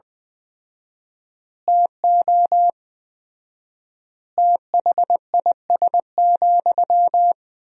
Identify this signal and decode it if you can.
Morse code, heard as a keyed tone, about 19 wpm, 700 Hz. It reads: TO THIS,
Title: 4x Word-Spacing - Sentences from Top 400 Words - 20wpm